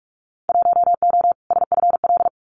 Morse code, heard as a keyed tone, about 44 wpm, 700 Hz. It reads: 1O SFL
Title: Winter Field Day Exchanges 45wpm